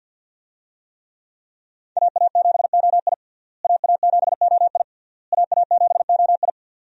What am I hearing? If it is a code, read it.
Morse code, heard as a keyed tone, about 52 wpm, 700 Hz. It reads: AA7OI AA7OI AA7OI
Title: USA Call Signs - 5 Characters - Repeated 3x - 50wpm